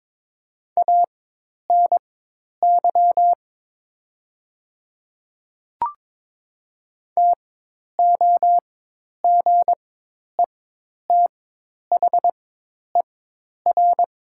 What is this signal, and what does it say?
Morse code, heard as a keyed tone, about 22 wpm, spaced as for 10 wpm, 700 Hz. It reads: ANY TOGETHER
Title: 4x Character Spacing - Top 300 Words - 22wpm